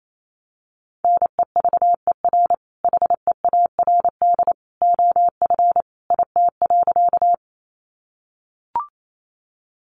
Morse code, heard as a keyed tone, about 28 wpm, 700 Hz. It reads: NEVER HEARD OF IT.
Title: Sentences from Top 400 Words 28wpm